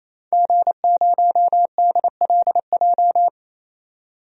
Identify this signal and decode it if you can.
Morse code, heard as a keyed tone, about 28 wpm, 700 Hz. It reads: G0DLJ
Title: DXCC-100 Call Signs - 5 Characters - 28wpm